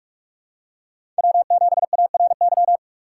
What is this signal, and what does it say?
Morse code, heard as a keyed tone, about 46 wpm, 700 Hz. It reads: W7ARY
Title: USA Call Signs - 6x Character Spacing - 45wpm